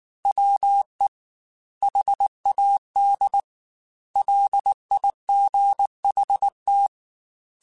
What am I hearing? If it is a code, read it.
Morse code, heard as a keyed tone, about 19 wpm, 800 Hz. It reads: WE HAD LIGHT
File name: Weather Sets / Weather Sets-mp3s / Weather 4 28 @18